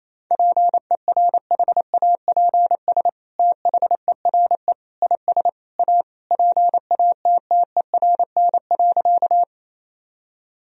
Morse code, heard as a keyed tone, about 27 wpm, 700 Hz. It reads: PERHAPS THERE IS A PATTERN.
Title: Sentences from Top 1000 Words - Review Part 2 - 28wpm